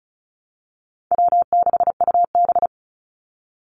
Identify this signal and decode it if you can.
Morse code, heard as a keyed tone, about 36 wpm, 700 Hz. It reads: W6UB
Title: NA SOTA Chaser Callsigns 35wpm